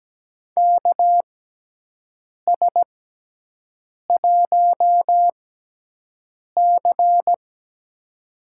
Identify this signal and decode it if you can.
Morse code, heard as a keyed tone, about 17 wpm, spaced as for 6 wpm, 700 Hz. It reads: KS1C
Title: USA Call Signs - 4 Characters - 6x Character Spacing - 17wpm